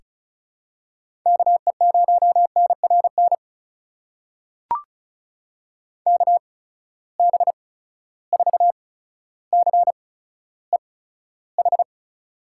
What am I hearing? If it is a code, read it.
Morse code, heard as a keyed tone, about 36 wpm, spaced as for 15 wpm, 700 Hz. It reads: KE0NRN KB4CEH
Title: USA Call Signs - 6 Characters - 8x Character Spacing - 35wpm